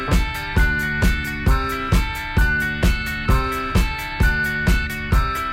Emergency Alert System (SAME) tones, heard as harsh buzzing bursts. It is heard from 0.6 to 4.6 s.